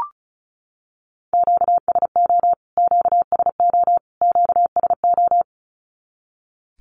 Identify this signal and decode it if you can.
Morse code, heard as a keyed tone, about 34 wpm, 700 Hz. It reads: QSO QSO QSO
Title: Q-Codes Repeated 3x 35wpm